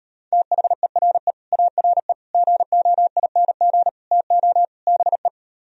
Morse code, heard as a keyed tone, about 38 wpm, 700 Hz. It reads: THERE ARE GOING TO BE